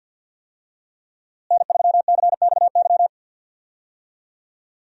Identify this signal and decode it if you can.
Morse code, heard as a keyed tone, about 51 wpm, 700 Hz. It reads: N3CXY
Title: USA Call Signs - 5 Characters - Repeated 3x - 50wpm